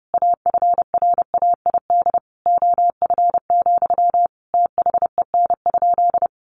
Morse code, heard as a keyed tone, about 30 wpm, 700 Hz. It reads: AFRAID OF, THEN?
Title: Sentences from Top 1500 Words 30wpm